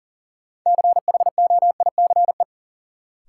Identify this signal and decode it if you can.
Morse code, heard as a keyed tone, about 37 wpm, 700 Hz. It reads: CHOICE
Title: Sets of 5 Words - New Word Drill - 40wpm